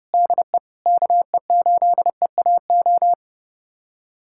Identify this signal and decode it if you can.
Morse code, heard as a keyed tone, about 32 wpm, 700 Hz. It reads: DE KE8EAO
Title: USA Callsign DE Callsign 30wpm